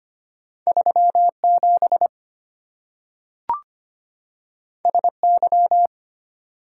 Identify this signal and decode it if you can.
Morse code, heard as a keyed tone, about 25 wpm, 700 Hz. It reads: 37 SY
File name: Two Letter-Number Combinations - 25wpm